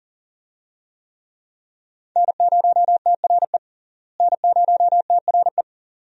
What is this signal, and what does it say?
Morse code, heard as a keyed tone, about 40 wpm, 700 Hz. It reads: N0TRE N0TRE
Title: USA Call Signs - 5 Characters - Repeated 2x - 40wpm